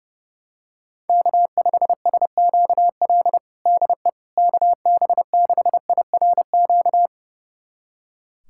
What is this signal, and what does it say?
Morse code, heard as a keyed tone, about 30 wpm, 700 Hz. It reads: K5SQL DE KB6IRQ
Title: USA Callsign DE Callsign 30wpm